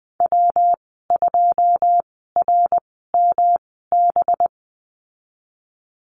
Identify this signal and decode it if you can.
Morse code, heard as a keyed tone, about 20 wpm, spaced as for 14 wpm, 700 Hz. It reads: W2RMB
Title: 219 Lesson 35g - Number 0 - Call Signs 20wpm